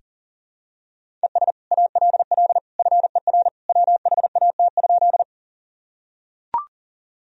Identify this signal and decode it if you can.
Morse code, heard as a keyed tone, about 40 wpm, 700 Hz. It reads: ES ALL FER WHAT?
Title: Sentences from Top 100 Words and Abbreviations 40wpm